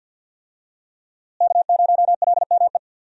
Morse code, heard as a keyed tone, about 51 wpm, 700 Hz. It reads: K0LME
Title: USA Call Signs - 5 Characters - Repeated 2x - 50wpm